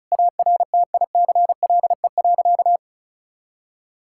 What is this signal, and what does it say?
Morse code, heard as a keyed tone, about 34 wpm, 690 Hz. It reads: ARTICLE.
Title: Sentences from Top 1000 Words - Review Part 5 - 35wpm